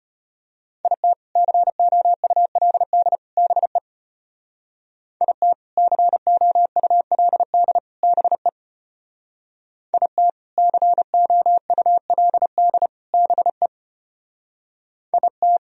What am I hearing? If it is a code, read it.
Morse code, heard as a keyed tone, about 33 wpm, 700 Hz. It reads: IT COULD BE IT COULD BE IT COULD BE IT